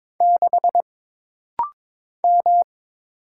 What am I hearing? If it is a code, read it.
Morse code, heard as a keyed tone, about 22 wpm, 700 Hz. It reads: 6 M
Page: This rapid-fire timing is geared to push your ICR (Instant Character Recognition) skills.